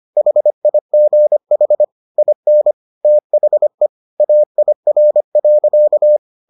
Morse code, heard as a keyed tone, about 25 wpm, 600 Hz. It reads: HIGH IN THE AIR.